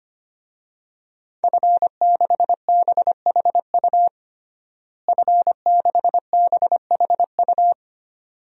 Morse code, heard as a keyed tone, about 25 wpm, 700 Hz. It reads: F6BHU F6BHU
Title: DXCC-100 Call Signs - 5 Characters - Repeated 2x - 25wpm